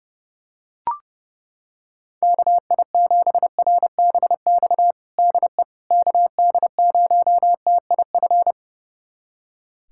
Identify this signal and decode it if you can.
Morse code, heard as a keyed tone, about 30 wpm, 700 Hz. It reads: KI7RBX DE KD0TIF
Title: USA Callsign DE Callsign - 6 Characters - 30wpm